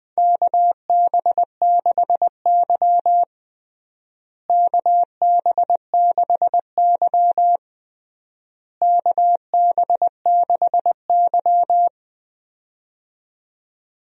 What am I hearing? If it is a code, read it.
Morse code, heard as a keyed tone, about 19 wpm, 700 Hz. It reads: KB6Y KB6Y KB6Y
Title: USA Call Signs - 4 Characters - Repeated 3x - 20wpm